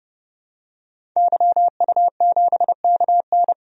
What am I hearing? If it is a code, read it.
Morse code, heard as a keyed tone, about 30 wpm, 700 Hz. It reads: YU7KN
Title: DXCC-100 Call Signs - 5 Characters - 30wpm